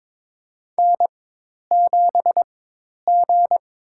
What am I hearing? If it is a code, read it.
Morse code, heard as a keyed tone, about 22 wpm, spaced as for 10 wpm, 700 Hz. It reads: N7G
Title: USA Call Signs - 4x Character Spacing - 22wpm